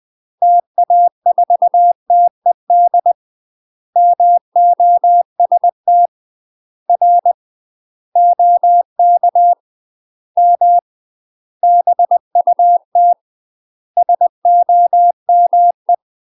Morse code, heard as a keyed tone, about 20 wpm, 700 Hz. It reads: TA4TED MOST R OK M BUT SOME
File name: UK MASTER QSOs TRAINING / 20WPM / UK MASTER QSOs 0025